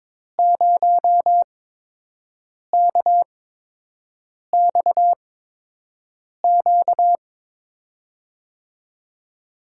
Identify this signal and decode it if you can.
Morse code, heard as a keyed tone, about 22 wpm, spaced as for 6 wpm, 700 Hz. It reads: 0KXQ
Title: USA Call Signs - 6 Characters - 8x Character Spacing - 22wpm